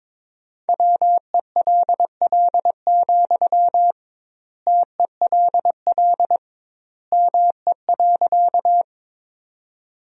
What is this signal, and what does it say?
Morse code, heard as a keyed tone, about 22 wpm, 700 Hz. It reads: WELL, TELL ME.
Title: 2x Word-Spacing - Sentences from Top 200 Words - 22wpm